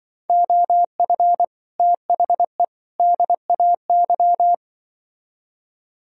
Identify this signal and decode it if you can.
Morse code, heard as a keyed tone, about 24 wpm, 700 Hz. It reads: OF THE DAY
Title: Speed-Racing - Sets of 3 Words 20wpm